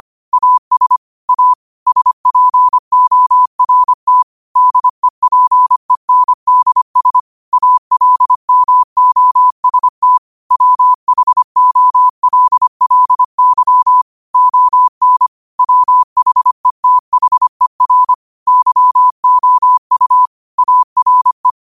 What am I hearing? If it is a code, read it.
Morse code, heard as a keyed tone, about 25 wpm, 1000 Hz. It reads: AS A SPORT DEPENDS ALMOST WHOLLY ON WHETHER YOU ARE
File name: qotdfiles / 25wpm